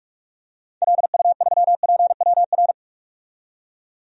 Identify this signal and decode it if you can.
Morse code, heard as a keyed tone, about 45 wpm, 700 Hz. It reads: LU3PWR
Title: DXCC-100 Call Signs - 6 Characters - 45wpm